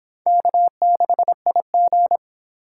Morse code, heard as a keyed tone, about 25 wpm, 700 Hz. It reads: K6IG